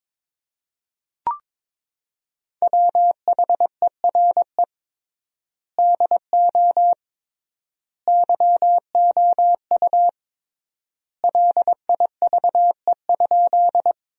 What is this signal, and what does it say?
Morse code, heard as a keyed tone, about 22 wpm, 700 Hz. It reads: WHERE DO YOU LIVE?
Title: 3x Word-Spacing - Sentences from Top 200 Words - 22wpm